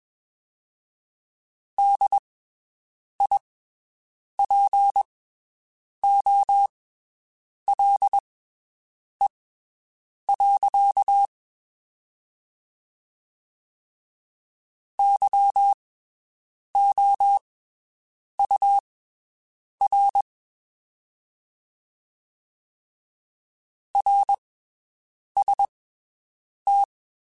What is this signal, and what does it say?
Morse code, heard as a keyed tone, about 21 wpm, spaced as for 7 wpm, 800 Hz. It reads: DIPOLE. YOUR RST
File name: (spaced) / QSOs Spaced-mp3s / WD0FBT 1 @20 sp